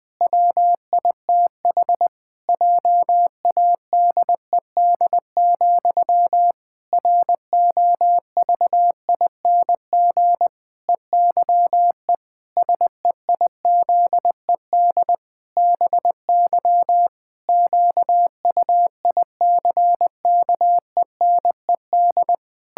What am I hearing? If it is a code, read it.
Morse code, heard as a keyed tone, about 20 wpm, 700 Hz. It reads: WITH JADED, ROVING EYE SEIZED BY QUICKENED
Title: Pangram Challenge v2 - 20wpm